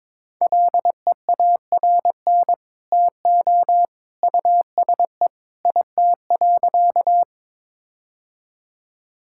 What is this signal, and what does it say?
Morse code, heard as a keyed tone, about 22 wpm, 700 Hz. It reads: LEARN TO USE IT.
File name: practice / 22wpm / Sentences with 2-Word Phrases - 22wpm